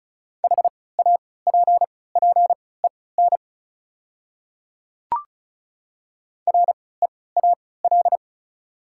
Morse code, heard as a keyed tone, about 36 wpm, spaced as for 20 wpm, 700 Hz. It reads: HAPPEN REAL
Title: 3x Character Spacing - Top 500 Words - 35wpm